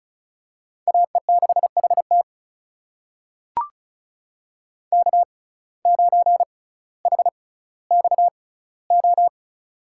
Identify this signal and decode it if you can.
Morse code, heard as a keyed tone, about 34 wpm, spaced as for 18 wpm, 700 Hz. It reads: AE6HT K9HXO
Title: USA Call Signs - 6x Character Spacing - 35wpm